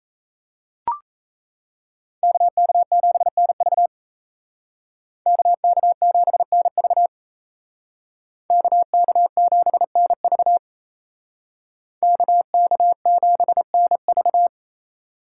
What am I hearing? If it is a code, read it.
Morse code, heard as a keyed tone, about 34 wpm, 700 Hz. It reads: KK7NV KK7NV KK7NV KK7NV
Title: Speed Racing - USA Call Signs 28wpm